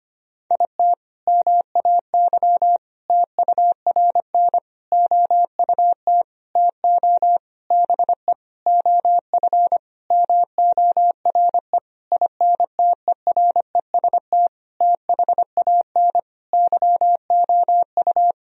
Morse code, heard as a keyed tone, about 25 wpm, 700 Hz. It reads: IT MAY TURN OUT TO BE OF MORE INTEREST THAN YOU
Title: Sentences from Top 1000 Words - Review Part 2 - 25wpm